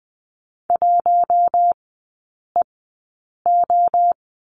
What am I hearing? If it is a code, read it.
Morse code, heard as a keyed tone, about 20 wpm, spaced as for 8 wpm, 700 Hz. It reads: 1EO
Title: 100 Lesson 17e - Letter u - Three Characters 20wpm